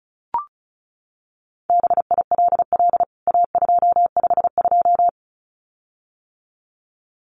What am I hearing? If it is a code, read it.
Morse code, heard as a keyed tone, about 34 wpm, 700 Hz. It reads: BILL A252
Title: CWT Exchange 35wpm